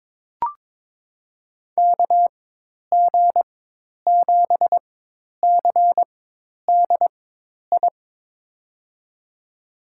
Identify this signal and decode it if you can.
Morse code, heard as a keyed tone, about 22 wpm, spaced as for 10 wpm, 700 Hz. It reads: KG7CDI
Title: USA Call Signs - 6 Characters - 4x Character Spacing - 22wpm